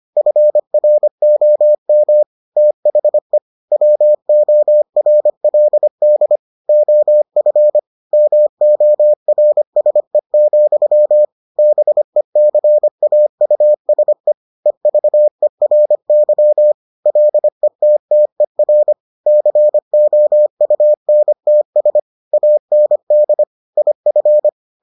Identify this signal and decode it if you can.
Morse code, heard as a keyed tone, about 25 wpm, 600 Hz. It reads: FROM THE WORLD OF MORSE, BECAUSE EVERY LETTER COUNTS AND IF